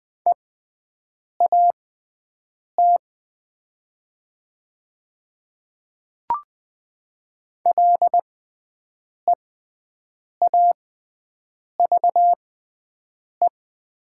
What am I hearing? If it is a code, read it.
Morse code, heard as a keyed tone, about 20 wpm, spaced as for 7 wpm, 700 Hz. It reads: EAT LEAVE
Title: 6x Character Spacing - Top 500 Words - 20wpm